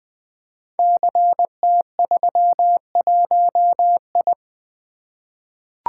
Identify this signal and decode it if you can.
Morse code, heard as a keyed tone, about 21 wpm, 700 Hz. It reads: CT31I
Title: Code Groups - 5 Letter:Numbers - 20wpm